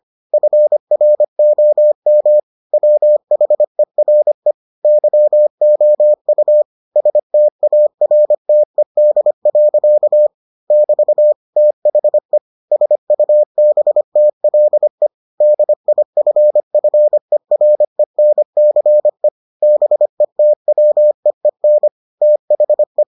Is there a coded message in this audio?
Yes: Morse code from 0.3 to 23.0 s, heard as a keyed tone, about 25 wpm, 600 Hz. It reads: FROM WHERE YOU STARTED. <BT> THE SUBTLE DIFFERENCE BETWEEN THE